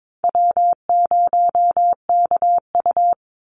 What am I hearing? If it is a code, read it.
Morse code, heard as a keyed tone, about 22 wpm, 700 Hz. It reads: W0KU
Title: Easy Call Signs Repeated 2x - 22wpm